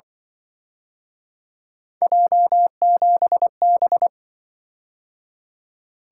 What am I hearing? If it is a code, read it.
Morse code, heard as a keyed tone, about 25 wpm, 700 Hz. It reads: J7B
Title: Speed-Racing - Three Letter-Number Combinations - 20wpm